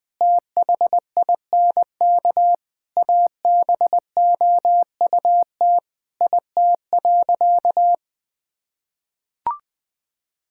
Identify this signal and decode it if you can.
Morse code, heard as a keyed tone, about 19 wpm, 700 Hz. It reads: THINK ABOUT IT.
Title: Sentences with 3-Word Phrases - 20wpm